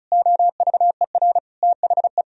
Morse code, heard as a keyed tone, about 43 wpm, 700 Hz. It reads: OVER THE
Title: Speed-Racing - Sets of 3 Words 35wpm